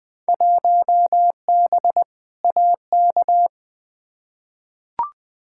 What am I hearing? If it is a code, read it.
Morse code, heard as a keyed tone, about 21 wpm, 700 Hz. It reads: 1B AK
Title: Field Day Exchanges Encore - 20wpm